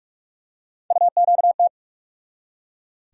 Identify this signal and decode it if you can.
Morse code, heard as a keyed tone, about 43 wpm, 700 Hz. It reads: UQT
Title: USA Call Sign Postfixes - 3 Letters 45wpm